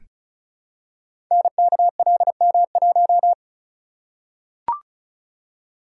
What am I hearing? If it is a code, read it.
Morse code, heard as a keyed tone, about 36 wpm, 700 Hz. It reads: NKLM1
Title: Code Groups - 5 Letter:Numbers - 35wpm